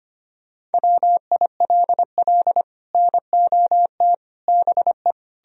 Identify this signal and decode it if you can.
Morse code, heard as a keyed tone, about 33 wpm, 700 Hz. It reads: WILL NOT BE